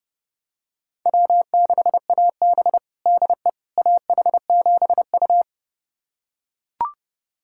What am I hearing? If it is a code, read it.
Morse code, heard as a keyed tone, about 32 wpm, 700 Hz. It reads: W6AB DE AH7U
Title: USA Callsign DE Callsign - 4 Characters - 30wpm